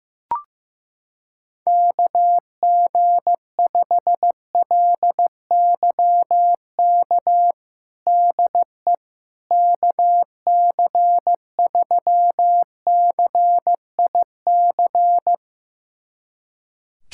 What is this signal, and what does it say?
Morse code, heard as a keyed tone, about 15 wpm, 700 Hz. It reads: KG5LYK DE KC3CIC
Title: USA Callsign DE Callsign - 6 Characters - 15wpm